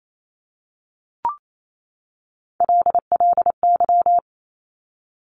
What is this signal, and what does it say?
Morse code, heard as a keyed tone, about 28 wpm, 700 Hz. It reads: LLY
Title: Top Three Letter Combinations in Words 28wpm